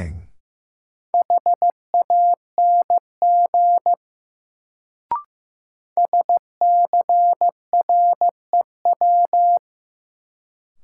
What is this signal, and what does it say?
Morse code, heard as a keyed tone, about 15 wpm, 700 Hz. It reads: HANG SCREW
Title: Sentences from Top 1600 Words - New Word Drill - 15wpm